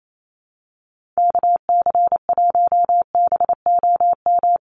Morse code, heard as a keyed tone, about 28 wpm, 700 Hz. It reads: KC1BOM
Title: USA Call Signs 28wpm